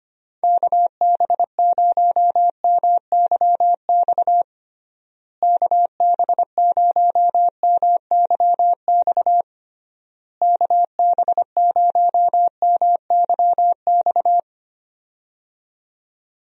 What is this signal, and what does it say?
Morse code, heard as a keyed tone, about 25 wpm, 700 Hz. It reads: KB0MYX KB0MYX KB0MYX
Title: USA Call Signs - 6 Characters - Repeated 3x - 25wpm